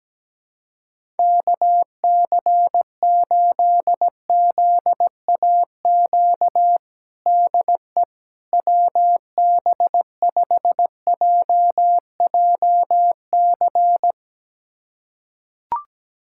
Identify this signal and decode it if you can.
Morse code, heard as a keyed tone, about 17 wpm, 700 Hz. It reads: KC8ZAQ DE WB5JJC
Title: USA Callsign DE Callsign 17wpm